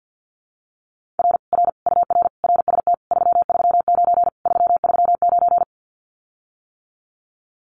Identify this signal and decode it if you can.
Morse code, heard as a keyed tone, about 50 wpm, 700 Hz. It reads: R R UR RST 339 339
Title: Common QSO Phrases Encore 50wpm